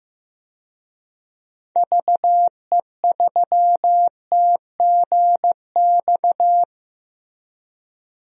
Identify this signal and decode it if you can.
Morse code, heard as a keyed tone, about 15 wpm, 700 Hz. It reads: VE3TGX
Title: DXCC-100 Call Signs - 6 Characters - 15wpm